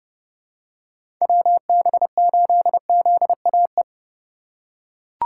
Morse code, heard as a keyed tone, about 32 wpm, 700 Hz. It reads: WB8ZAE
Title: USA Call Signs - 6 Characters - 3x Character Spacing - 30wpm